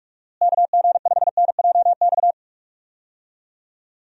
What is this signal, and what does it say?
Morse code, heard as a keyed tone, about 46 wpm, 700 Hz. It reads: KG5NJX
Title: USA Call Signs - 6 Characters - Repeated 2x - 45wpm